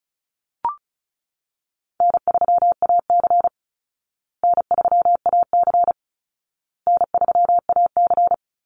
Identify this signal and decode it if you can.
Morse code, heard as a keyed tone, about 34 wpm, 700 Hz. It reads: N3AC N3AC N3AC
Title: Call Signs Repeated 2x - 35wpm